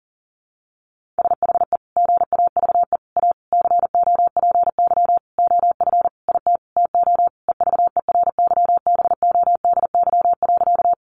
Morse code, heard as a keyed tone, about 40 wpm, 700 Hz. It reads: SHE GAVE A COPY OF IT TO EVERYBODY.
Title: Sentences from Top 1400 Words 40wpm